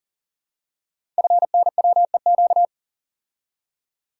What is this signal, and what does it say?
Morse code, heard as a keyed tone, about 38 wpm, 690 Hz. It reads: FNWEQ
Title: Code Groups - 5 Letters - 40wpm